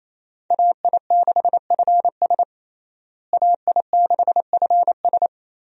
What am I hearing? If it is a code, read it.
Morse code, heard as a keyed tone, about 28 wpm, 700 Hz. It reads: AI6FS AI6FS
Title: USA Call Signs - 5 Characters - Repeated 3x - 28wpm